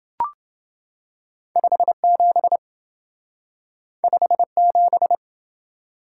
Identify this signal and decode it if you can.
Morse code, heard as a keyed tone, about 27 wpm, 700 Hz. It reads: 57 57